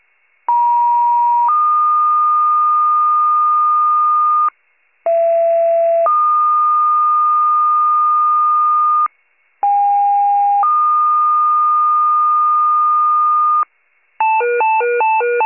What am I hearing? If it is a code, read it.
Two-tone page, heard as a one-second tone, then a three-second tone, about 950 then 1220 Hz, about 670 then 1220 Hz, about 810 then 1220 Hz. It is heard from 0.5 to 13.6 s.
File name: wcalert